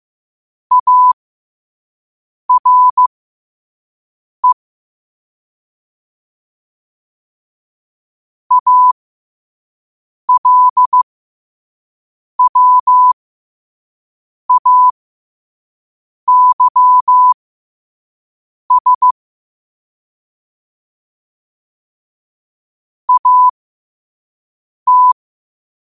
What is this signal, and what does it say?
Morse code, heard as a keyed tone, about 15 wpm, spaced as for 5 wpm, 1000 Hz. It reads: ARE ALWAYS AT
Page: New quotes every day in morse code at 5 Words per minute.